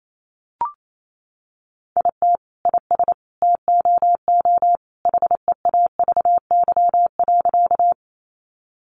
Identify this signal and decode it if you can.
Morse code, heard as a keyed tone, about 28 wpm, 700 Hz. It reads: IT IS TOO HEAVY.
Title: Sentences from Top 1100 Words 28wpm